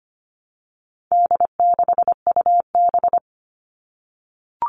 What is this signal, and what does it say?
Morse code, heard as a keyed tone, about 25 wpm, 700 Hz. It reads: D6UB
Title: All Cut Numbers - 4-Digit 25wpm